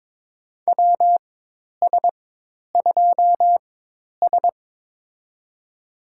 Morse code, heard as a keyed tone, about 22 wpm, spaced as for 10 wpm, 700 Hz. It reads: WS2S
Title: USA Call Signs - 4 Characters - 4x Character Spacing - 22wpm